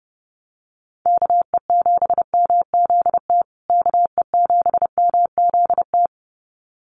Morse code, heard as a keyed tone, about 30 wpm, 700 Hz. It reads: KE7MZT KE7MZT
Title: USA Call Signs Repeated 2x 30wpm